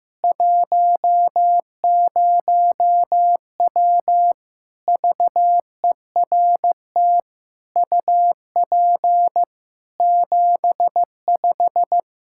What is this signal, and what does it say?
Morse code, heard as a keyed tone, about 16 wpm, 700 Hz. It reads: 10W VERT UP 75
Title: Practice QSO - Exchange 2 - Info - 15wpm